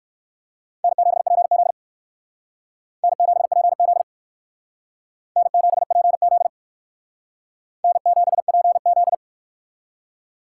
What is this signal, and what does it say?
Morse code, heard as a keyed tone, about 56 wpm, 700 Hz. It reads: N7PZ N7PZ N7PZ N7PZ